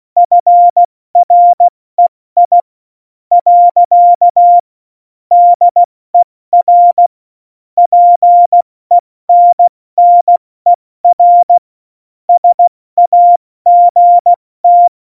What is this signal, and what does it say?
Morse code, heard as a keyed tone, about 16 wpm, spaced as for 13 wpm, 700 Hz. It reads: FREI. DER PENNER SAGT
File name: Morsekurs / Witze / Witz 07. Januar 2020, 70 F80